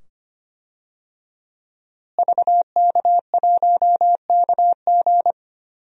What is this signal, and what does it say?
Morse code, heard as a keyed tone, about 25 wpm, 700 Hz. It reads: VK1KG
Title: DXCC-100 Call Signs - 5 Characters - Repeated 2x - 25wpm